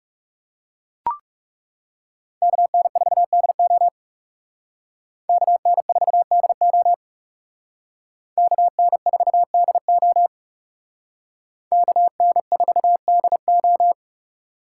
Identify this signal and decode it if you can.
Morse code, heard as a keyed tone, about 34 wpm, 700 Hz. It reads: KN4DO KN4DO KN4DO KN4DO
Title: Speed Racing - USA 2x2 Call Signs 30wpm